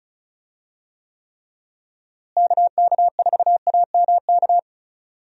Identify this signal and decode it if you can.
Morse code, heard as a keyed tone, about 34 wpm, 700 Hz. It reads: KK4AMK
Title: USA Call Signs - 6 Characters - Repeated 3x - 35wpm